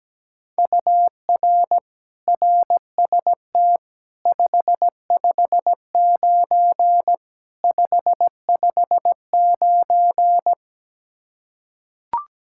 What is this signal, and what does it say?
Morse code, heard as a keyed tone, about 17 wpm, 700 Hz. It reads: UR RST 559 559
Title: Practice QSO - Exchange 1 - Beginning - 17wpm